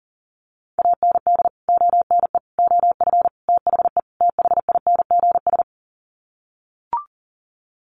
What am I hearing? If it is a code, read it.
Morse code, heard as a keyed tone, about 40 wpm, 700 Hz. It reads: AND ONE OF THE THINGS
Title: Sets of 5 Words 40wpm